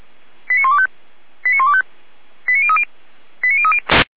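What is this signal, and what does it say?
Five-tone selective call (ZVEI), heard as quick runs of five tones, tones of about 70 ms. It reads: 89125 89125 89030 89030
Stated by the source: ZVEI five-tone selective call received by SDR.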